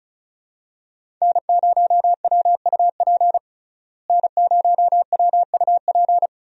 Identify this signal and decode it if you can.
Morse code, heard as a keyed tone, about 34 wpm, 700 Hz. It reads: N0WUP N0WUP
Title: USA Call Signs - 5 Characters - Repeated 2x - 35wpm